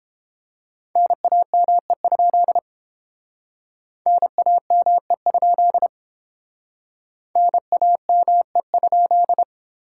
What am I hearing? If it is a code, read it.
Morse code, heard as a keyed tone, about 29 wpm, 700 Hz. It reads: NAME? NAME? NAME?